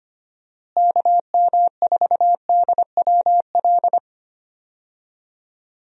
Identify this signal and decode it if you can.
Morse code, heard as a keyed tone, about 25 wpm, 700 Hz. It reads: KM4DWL
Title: USA Call Signs - 6 Characters - Repeated 2x - 25wpm